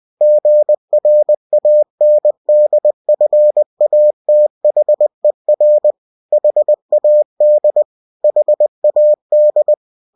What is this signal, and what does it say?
Morse code, heard as a keyed tone, about 20 wpm, 600 Hz. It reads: GRANDFATHER HAD HAD
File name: cw-books / winnie-the-pooh / winnie-20wpm-page-0020